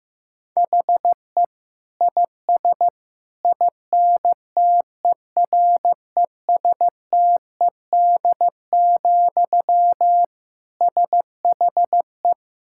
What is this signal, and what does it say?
Morse code, heard as a keyed tone, about 15 wpm, 700 Hz. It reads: HE IS INTERESTED, SHE
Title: Sentences from Top 1000 Words - Review Part 6 - 15wpm